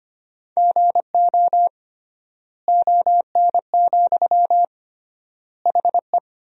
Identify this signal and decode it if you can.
Morse code, heard as a keyed tone, about 25 wpm, 700 Hz. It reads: GO ON, HE
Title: 3x Word-Spacing - Sentences from Top 100 Words - 25wpm